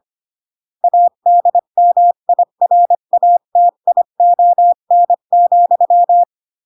Morse code, heard as a keyed tone, about 25 wpm, 700 Hz. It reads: ADMIRATION,
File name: CW / jodorowsky / 82 commandments 25-20